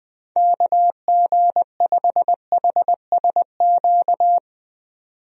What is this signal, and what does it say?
Morse code, heard as a keyed tone, about 20 wpm, 700 Hz. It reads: KG5HSQ
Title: USA Call Signs - 6 Characters - 20wpm